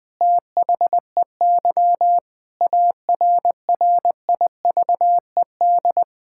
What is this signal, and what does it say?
Morse code, heard as a keyed tone, about 21 wpm, 700 Hz. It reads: THEY ARRIVED
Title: Sentences with 4-Word Phrases - 20wpm